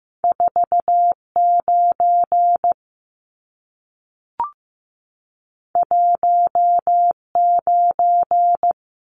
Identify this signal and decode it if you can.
Morse code, heard as a keyed tone, about 15 wpm, 700 Hz. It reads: 4 9 1 9
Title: Numbers - 2-Digit 15wpm